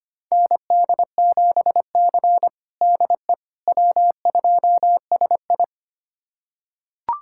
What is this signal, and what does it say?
Morse code, heard as a keyed tone, about 25 wpm, 700 Hz. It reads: ND7C DE W2SI
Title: USA Callsign DE Callsign - 4 Characters - 25wpm